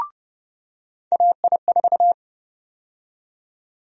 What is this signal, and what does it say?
Morse code, heard as a keyed tone, about 30 wpm, 700 Hz. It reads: AI4
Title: USA Call Sign Prefixes - 30wpm